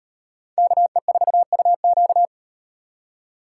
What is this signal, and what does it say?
Morse code, heard as a keyed tone, about 37 wpm, 700 Hz. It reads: KE4UQ